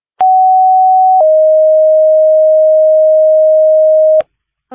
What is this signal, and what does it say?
Two-tone page, heard as a one-second tone, then a three-second tone, about 750 then 620 Hz. It is heard from 0.2 to 4.2 s.